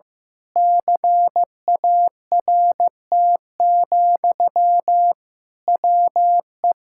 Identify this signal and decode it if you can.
Morse code, heard as a keyed tone, about 15 wpm, 700 Hz. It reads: CART, WE
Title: Pangram Challenge v2 - 15wpm